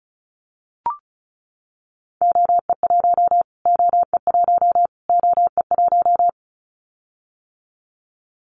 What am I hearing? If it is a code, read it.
Morse code, heard as a keyed tone, about 34 wpm, 700 Hz. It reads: OE1 OE1 OE1
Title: Callsign Prefixes for Top 25 DXCC - Repeated 3x - 35wpm